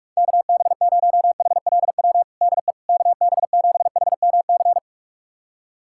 Morse code, heard as a keyed tone, about 44 wpm, 700 Hz. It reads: KB0HLW DE KB7HMC
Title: USA Callsign DE Callsign - 6 Characters - 45wpm